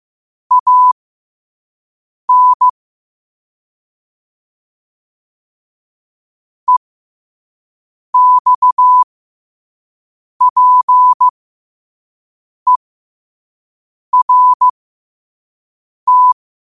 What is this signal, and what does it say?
Morse code, heard as a keyed tone, about 15 wpm, spaced as for 5 wpm, 1000 Hz. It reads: AN EXPERT
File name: qotdfiles / 5wpm